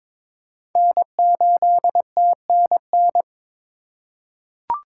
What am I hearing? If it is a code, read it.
Morse code, heard as a keyed tone, about 22 wpm, 700 Hz. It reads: N8TNN
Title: USA Call Signs - 2x Character Spacing - 22wpm